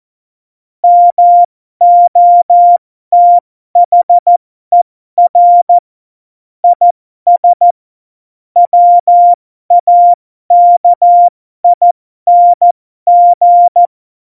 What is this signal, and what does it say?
Morse code, heard as a keyed tone, about 14 wpm, spaced as for 12 wpm, 700 Hz. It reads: MOTHER IS WAKING
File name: Morsekurs / Witze